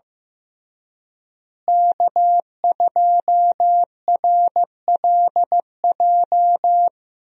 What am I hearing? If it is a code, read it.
Morse code, heard as a keyed tone, about 15 wpm, 700 Hz. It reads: K2RLJ
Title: USA Call Signs - 5 Characters - Repeated 2x - 15wpm